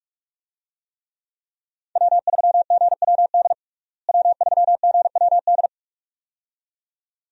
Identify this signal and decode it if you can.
Morse code, heard as a keyed tone, about 46 wpm, 700 Hz. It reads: W3GWD W3GWD
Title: USA Call Signs - 5 Characters - Repeated 2x - 45wpm